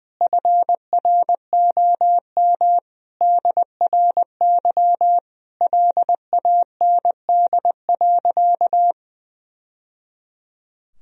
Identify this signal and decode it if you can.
Morse code, heard as a keyed tone, about 20 wpm, 700 Hz. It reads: FROM DRY LAND.
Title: Sentences from Top 1000 Words - Review Part 3 - 20wpm